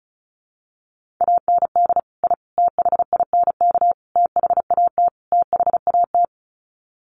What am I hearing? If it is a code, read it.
Morse code, heard as a keyed tone, about 34 wpm, 700 Hz. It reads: AND I THINK THAT THAT
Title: Sets of 5 Words Encore 35wpm